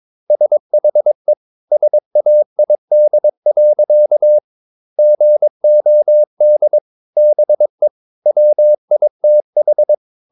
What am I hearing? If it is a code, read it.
Morse code, heard as a keyed tone, about 22 wpm, 600 Hz. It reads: SHE SAID. GOD BE WITH
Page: Romanen Never af Ken Follet på engelsk.